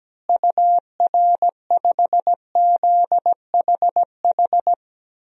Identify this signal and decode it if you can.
Morse code, heard as a keyed tone, about 17 wpm, 700 Hz. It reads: UR5ZHH
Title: DXCC-100 Call Signs - 6 Characters - Repeated 3x - 17wpm